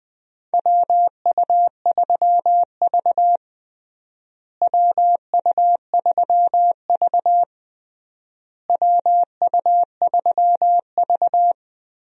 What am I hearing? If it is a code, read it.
Morse code, heard as a keyed tone, about 21 wpm, 700 Hz. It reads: WU3V WU3V WU3V
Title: USA Call Signs - 4 Characters - Repeated 3x - 20wpm